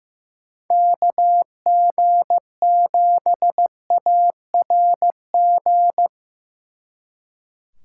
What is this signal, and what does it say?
Morse code, heard as a keyed tone, about 18 wpm, 700 Hz. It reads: KG7ARG
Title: Speed Racing - USA 2x3 Call Signs 15wpm